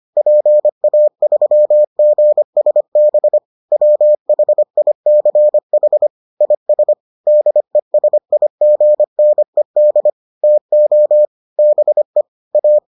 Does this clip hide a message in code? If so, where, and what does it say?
Yes: Morse code from 0.2 to 12.8 s, heard as a keyed tone, about 25 wpm, 600 Hz. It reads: PA3GSB WHICH IS DESIGNED TO BE A